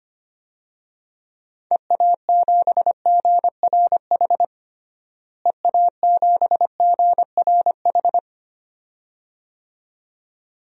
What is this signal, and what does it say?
Morse code, heard as a keyed tone, about 25 wpm, 700 Hz. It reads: EA7GRH EA7GRH
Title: DXCC-100 Call Signs - 6 Characters - Repeated 2x - 25wpm